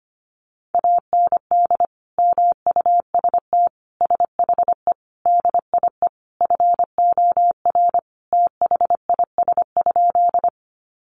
Morse code, heard as a keyed tone, about 25 wpm, 700 Hz. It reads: AND MUST SHE DIE FOR THIS?
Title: Sentences from Top 1300 Words 25wpm